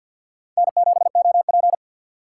Morse code, heard as a keyed tone, about 51 wpm, 690 Hz. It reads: N7OP
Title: USA Call Signs - 4 Characters - 6x Character Spacing - 50wpm